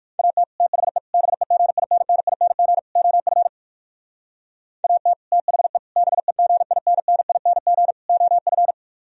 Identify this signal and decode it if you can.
Morse code, heard as a keyed tone, about 51 wpm, 700 Hz. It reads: AT THE BEGINNING OF AT THE BEGINNING OF
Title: Speed-Racing - Sets of 4 Words 45wpm